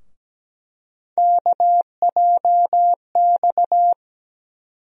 Morse code, heard as a keyed tone, about 17 wpm, 700 Hz. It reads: KJX
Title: USA Call Sign Postfixes - 3 Letters 17wpm